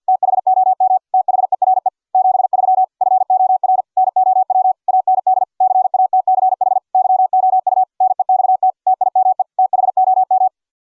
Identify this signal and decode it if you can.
Morse code, heard as a keyed tone, about 48 wpm, 750 Hz. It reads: THOM THERE 73 FOR NOW AND CATCH YOU NEXT TIME THOM